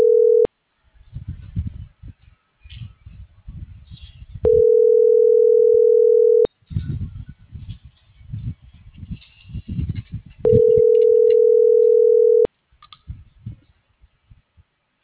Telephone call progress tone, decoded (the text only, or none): ringback tone